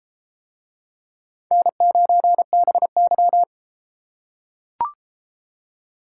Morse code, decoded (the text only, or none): N9BY